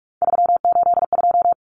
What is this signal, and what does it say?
Morse code, heard as a keyed tone, about 44 wpm, 700 Hz. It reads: 382
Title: Common Cut Numbers - 3-Digit 45wpm